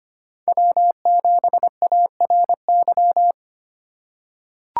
Morse code, heard as a keyed tone, about 25 wpm, 700 Hz. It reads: W7ARY
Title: USA Call Signs - 4x Character Spacing - 25wpm